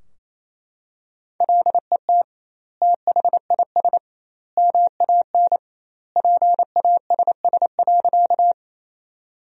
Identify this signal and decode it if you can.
Morse code, heard as a keyed tone, about 28 wpm, 700 Hz. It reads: LET THIS MAN PASS.
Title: 2x Word-Spacing - Sentences from Top 400 Words - 28wpm